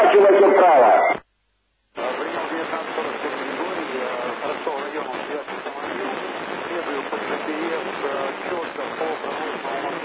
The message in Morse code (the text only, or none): none